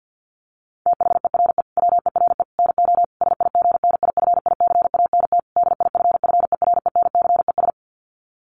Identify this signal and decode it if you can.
Morse code, heard as a keyed tone, about 50 wpm, 700 Hz. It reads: THERE WERE NO SIGNIFICANT DIFFERENCES
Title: Sets of 5 Words Encore2 50wpm